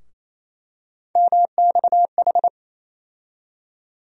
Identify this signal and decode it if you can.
Morse code, heard as a keyed tone, about 28 wpm, 700 Hz. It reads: MXH
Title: USA Call Sign Postfixes - 3 Letters 28wpm